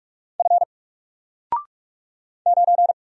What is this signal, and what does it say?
Morse code, heard as a keyed tone, about 44 wpm, 700 Hz. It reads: F 9
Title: Single Character (Letter:Number:Punctuation:Prosign) 45wpm